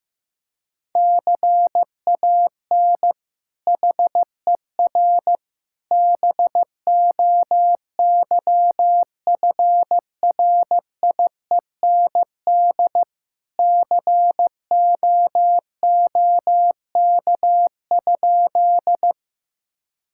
Morse code, heard as a keyed tone, about 15 wpm, 700 Hz. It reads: CAN HER BOYFRIEND COOK?
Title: Sentences from Top 1000 Words - Review Part 5 - 15wpm